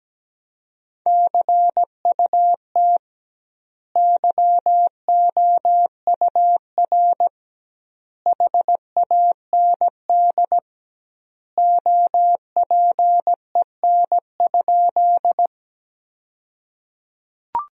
Morse code, heard as a keyed tone, about 17 wpm, 700 Hz. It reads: CUT YOUR HAND OPEN?
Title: 2x Word-Spacing - Sentences from Top 300 Words - 17wpm